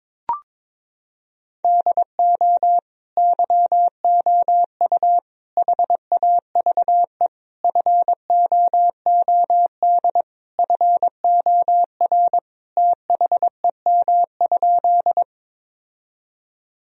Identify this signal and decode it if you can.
Morse code, heard as a keyed tone, about 22 wpm, 700 Hz. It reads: DO YOU HAVE FOOD FOR THEM?
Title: Sentences from Top 1000 Words - Review Part 1 - 22wpm